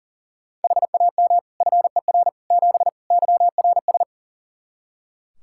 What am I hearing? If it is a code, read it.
Morse code, heard as a keyed tone, about 40 wpm, 700 Hz. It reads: HAM FER 7 YRS
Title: Practice QSO - Exchange 3 - More Info - 40wpm